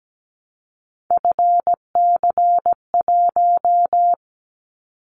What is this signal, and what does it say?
Morse code, heard as a keyed tone, about 17 wpm, 700 Hz. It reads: FC1
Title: Callsign Prefixes for Top 25 DXCC - 17wpm